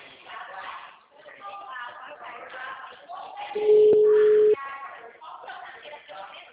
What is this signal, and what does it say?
Ringback tone (Europe, 425 Hz), with speech.
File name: out- -2014- - - .76307